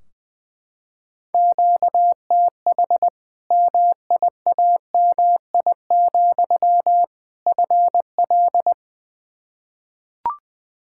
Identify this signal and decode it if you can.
Morse code, heard as a keyed tone, about 20 wpm, 700 Hz. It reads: QTH MIAMI, FL
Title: Practice QSO - Exchange 1 - Beginning - 20wpm